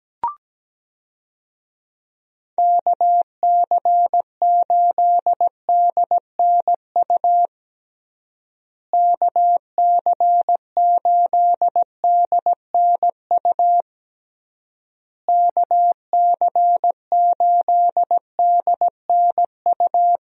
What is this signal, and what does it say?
Morse code, heard as a keyed tone, about 17 wpm, 700 Hz. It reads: KC8DNU KC8DNU KC8DNU
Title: USA Call Signs - 6 Characters - Repeated 3x - 17wpm